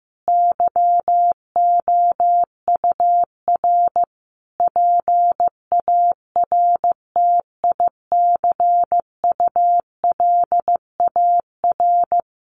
Morse code, heard as a keyed tone, about 14 wpm, 700 Hz. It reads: YOUR PARTICULAR
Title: Sentences from Top 800 Words 15wpm